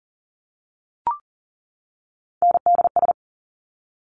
Morse code, heard as a keyed tone, about 38 wpm, 700 Hz. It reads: NDS
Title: Top Three Letter Combinations in Words 40wpm